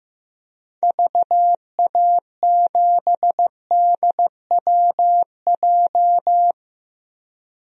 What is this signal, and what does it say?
Morse code, heard as a keyed tone, about 15 wpm, 700 Hz. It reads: VA7DWJ
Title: DXCC-100 Call Signs - 6 Characters - 15wpm